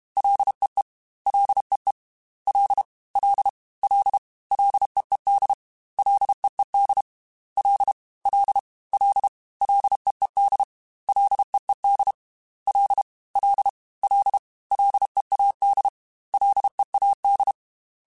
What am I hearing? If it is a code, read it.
Morse code, heard as a keyed tone, about 32 wpm, 800 Hz. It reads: LEE LEE L L L LEED LEED L L L LEED LEED L L L LEAD LEAD
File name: Ls / Single L m3s / Single L Prefixe 3x2 Twice @30